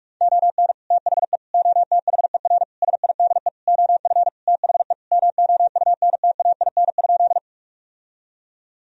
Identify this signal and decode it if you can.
Morse code, heard as a keyed tone, about 42 wpm, 700 Hz. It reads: ON THE OTHER SIDE OF THE MOUNTAIN?
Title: Sentences from Top 1000 Words - Review Part 1 - 45wpm